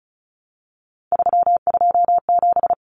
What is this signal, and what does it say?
Morse code, heard as a keyed tone, about 34 wpm, 700 Hz. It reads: 327
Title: Common Cut Numbers - 3-Digit 35wpm